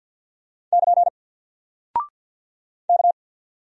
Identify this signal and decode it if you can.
Morse code, heard as a keyed tone, about 48 wpm, 700 Hz. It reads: <KN> K
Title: Single Character (Letter:Number:Punctuation:Prosign) Alt 50wpm